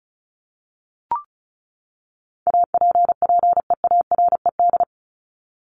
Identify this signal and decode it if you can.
Morse code, heard as a keyed tone, about 34 wpm, 700 Hz. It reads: APPEARED
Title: ED Suffix - Top 100 Words with Base Words 35wpm